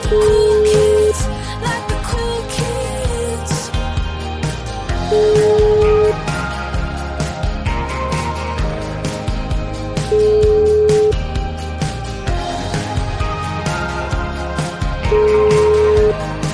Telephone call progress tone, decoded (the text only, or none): ringback tone